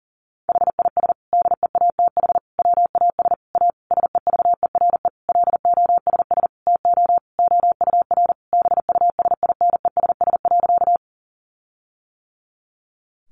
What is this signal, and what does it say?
Morse code, heard as a keyed tone, about 40 wpm, 700 Hz. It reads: HIS DEATH WAS A SEVERE LOSS TO OUR BUSINESS.
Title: Sentences from Top 1400 Words 40wpm